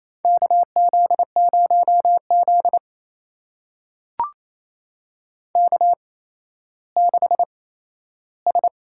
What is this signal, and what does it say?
Morse code, heard as a keyed tone, about 29 wpm, spaced as for 14 wpm, 700 Hz. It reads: KZ0Z K6S
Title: USA Call Signs - 4 Characters - 8x Character Spacing - 28wpm